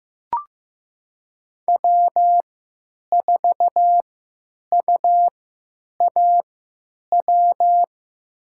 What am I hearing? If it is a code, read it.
Morse code, heard as a keyed tone, about 15 wpm, spaced as for 9 wpm, 700 Hz. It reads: W4UAW
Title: USA Call Signs - 3x Character Spacing - 15wpm